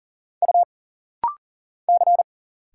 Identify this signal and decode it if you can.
Morse code, heard as a keyed tone, about 39 wpm, 700 Hz. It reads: U C
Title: Single Letters - Rapid-Fire 40wpm